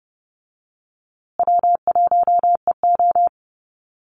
Morse code, heard as a keyed tone, about 30 wpm, 700 Hz. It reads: W1EO
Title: USA Call Signs Repeated 3x 30wpm